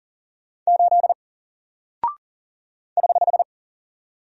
Morse code, heard as a keyed tone, about 40 wpm, 700 Hz. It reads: 8 <HH>
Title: Single Character (Letter:Number:Punctuation:Prosign) Alt 40wpm